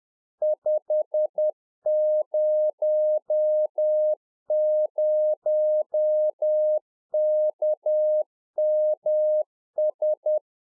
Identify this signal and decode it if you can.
Morse code, heard as a keyed tone, about 10 wpm, 610 Hz. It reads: 500KMS